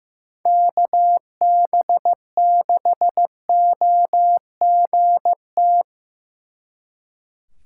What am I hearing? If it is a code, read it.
Morse code, heard as a keyed tone, about 18 wpm, 700 Hz. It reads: KB6OGT
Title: Speed Racing - USA 2x3 Call Signs 15wpm